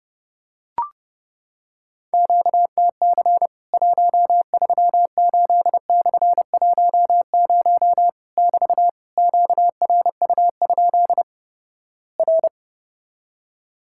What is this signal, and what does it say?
Morse code, heard as a keyed tone, about 31 wpm, 700 Hz. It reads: QTC 138/10 <BT> QRU? R
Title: Worked All Europe DX Contest - QTC Exchange - 30wpm